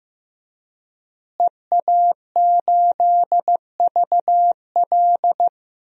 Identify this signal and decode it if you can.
Morse code, heard as a keyed tone, about 15 wpm, 700 Hz. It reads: EA8VL
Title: DXCC-100 Call Signs - 5 Characters - Repeated 3x - 15wpm